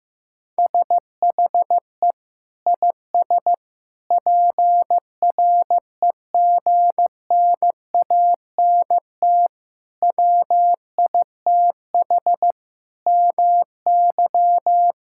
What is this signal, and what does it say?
Morse code, heard as a keyed tone, about 15 wpm, 700 Hz. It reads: SHE IS PREGNANT WITH MY
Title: Sentences from Top 1000 Words - Review Part 6 - 15wpm